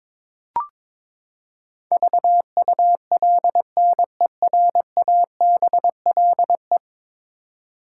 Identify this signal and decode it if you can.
Morse code, heard as a keyed tone, about 22 wpm, 700 Hz. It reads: VULNERABLE
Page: morse-code-ninja-practice-22wpm